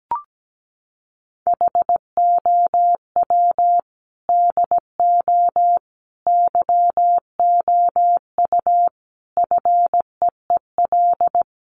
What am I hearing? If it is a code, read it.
Morse code, heard as a keyed tone, about 17 wpm, 700 Hz. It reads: HOW DO YOU FEEL
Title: Sets of 5 Words 17wpm